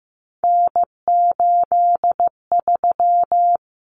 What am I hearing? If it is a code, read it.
Morse code, heard as a keyed tone, about 15 wpm, 700 Hz. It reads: N83
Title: Common Cut Numbers - 3-Digit 15wpm